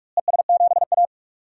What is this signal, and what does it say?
Morse code, heard as a keyed tone, about 46 wpm, 700 Hz. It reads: ES7A
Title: DXCC-100 Call Signs - 4 Characters - Repeated 3x - 45wpm